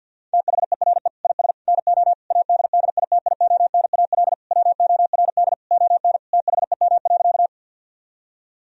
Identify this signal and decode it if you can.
Morse code, heard as a keyed tone, about 50 wpm, 700 Hz. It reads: THERE IS NO ADDITIONAL WORD ON THEM.
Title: Sentences from Top 1000 Words - Review Part 4 - 50wpm